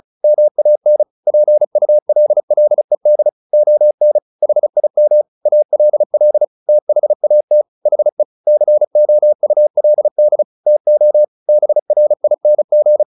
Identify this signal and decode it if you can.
Morse code, heard as a keyed tone, about 34 wpm, 600 Hz. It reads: MAN PULLED ON HIM ALL THAT HE COULD TO BRING
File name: the old man and the sea 2 35wpm